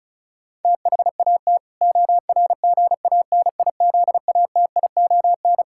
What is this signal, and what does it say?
Morse code, heard as a keyed tone, about 36 wpm, 700 Hz. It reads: THAT ORGANIZATION
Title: Sentences from Top 1000 Words - Review Part 6 - 35wpm